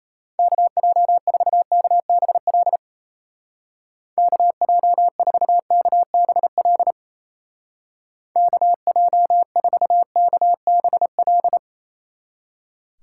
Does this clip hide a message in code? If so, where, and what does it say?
Yes: Morse code from 0.4 to 11.6 s, heard as a keyed tone, about 36 wpm, 700 Hz. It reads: KJ4KBL KJ4KBL KJ4KBL